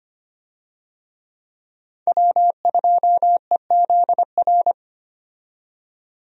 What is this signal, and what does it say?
Morse code, heard as a keyed tone, about 25 wpm, 700 Hz. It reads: W2EZR
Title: USA Call Signs - 5 Characters - 25wpm